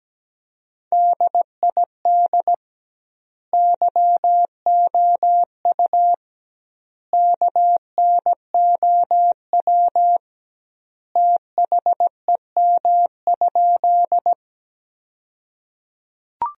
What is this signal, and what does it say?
Morse code, heard as a keyed tone, about 17 wpm, 700 Hz. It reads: DID YOU KNOW THEM?
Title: 2x Word-Spacing - Sentences from Top 100 Words - 17wpm